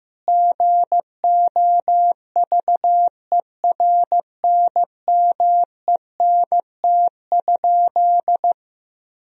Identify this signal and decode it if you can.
Morse code, heard as a keyed tone, about 15 wpm, 700 Hz. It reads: GOVERNMENT?
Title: Sentences from Top 1000 Words - Review Part 4 - 15wpm